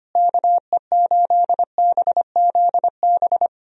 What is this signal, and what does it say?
Morse code, heard as a keyed tone, about 25 wpm, 700 Hz. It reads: KE8BZB
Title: USA Call Signs - 6 Characters - Repeated 3x - 25wpm